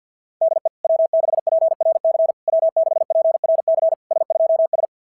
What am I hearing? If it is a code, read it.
Morse code, heard as a keyed tone, about 50 wpm, 650 Hz. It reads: DE W6PRC W6PRC S2S
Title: SOTA QSO Practice 50wpm